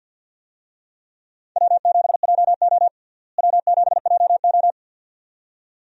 Morse code, heard as a keyed tone, about 52 wpm, 700 Hz. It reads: W7JO W7JO
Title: USA Call Signs - 4 Characters - Repeated 2x - 50wpm